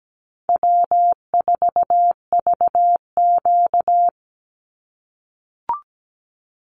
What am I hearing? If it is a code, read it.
Morse code, heard as a keyed tone, about 17 wpm, 700 Hz. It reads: W4VQ
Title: Easy Call Signs - 17wpm